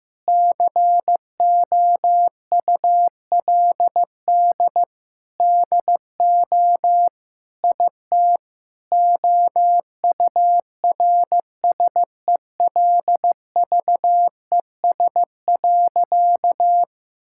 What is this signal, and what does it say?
Morse code, heard as a keyed tone, about 15 wpm, 700 Hz. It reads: COULD DO IT OURSELVES.
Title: Sentences from Top 1600 Words 15wpm